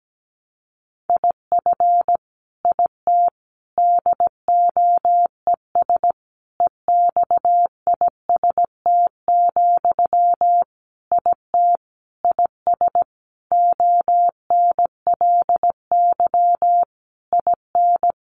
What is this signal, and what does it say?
Morse code, heard as a keyed tone, about 17 wpm, 700 Hz. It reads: IF IT DOES EXIST, IT IS ONLY IN
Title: Sentences from Top 1300 Words 17wpm